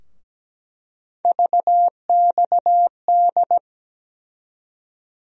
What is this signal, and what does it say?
Morse code, heard as a keyed tone, about 17 wpm, 700 Hz. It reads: VXD
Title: USA Call Sign Postfixes - 3 Letters 17wpm